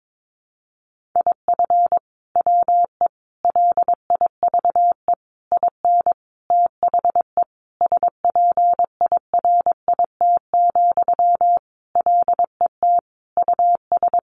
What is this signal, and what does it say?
Morse code, heard as a keyed tone, about 22 wpm, 700 Hz. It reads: IF WE LIVE IN THE SPIRIT, LET US
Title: Sentences from Top 1400 Words 22wpm